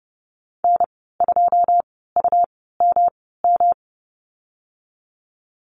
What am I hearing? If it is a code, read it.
Morse code, heard as a keyed tone, about 30 wpm, spaced as for 17 wpm, 700 Hz. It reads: N2UMM
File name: Lesson 19g - Letter w - Call Signs 30wpm